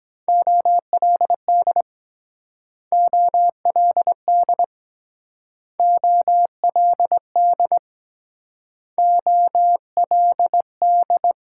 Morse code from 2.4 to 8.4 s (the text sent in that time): OLD OLD